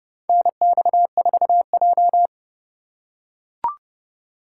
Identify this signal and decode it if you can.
Morse code, heard as a keyed tone, about 32 wpm, 700 Hz. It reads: NX4J
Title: USA Call Signs - 4 Characters - 2x Character Spacing - 30wpm